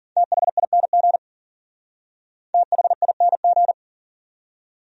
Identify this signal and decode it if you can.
Morse code, heard as a keyed tone, about 47 wpm, 700 Hz. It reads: THING THING
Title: Speed-Racing Top 200 Words 40wpm